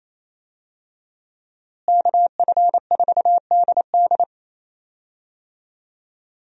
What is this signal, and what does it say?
Morse code, heard as a keyed tone, about 28 wpm, 700 Hz. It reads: KF4DD
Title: USA Call Signs - 5 Characters - Repeated 3x - 28wpm